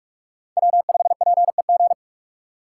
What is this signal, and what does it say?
Morse code, heard as a keyed tone, about 46 wpm, 700 Hz. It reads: W5PEG
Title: USA Call Signs - 5 Characters - 3x Character Spacing - 45wpm